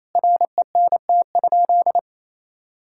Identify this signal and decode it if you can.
Morse code, heard as a keyed tone, about 28 wpm, 700 Hz. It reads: RENT?
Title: Sentences from Top 1000 Words - Review Part 5 - 28wpm